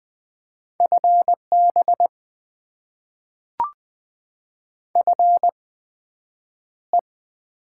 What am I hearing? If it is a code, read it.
Morse code, heard as a keyed tone, about 20 wpm, spaced as for 9 wpm, 700 Hz. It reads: FB FE
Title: Top 20 QSO Elements - 8x Character Spacing - 20wpm